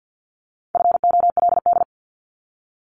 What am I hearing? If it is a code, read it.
Morse code, heard as a keyed tone, about 50 wpm, 700 Hz. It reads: FOLD
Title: Top 1751-2000 Words 50wpm